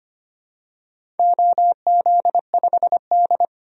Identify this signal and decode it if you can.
Morse code, heard as a keyed tone, about 25 wpm, 700 Hz. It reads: OZ5D
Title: DXCC-100 Call Signs - 4 Characters - 25wpm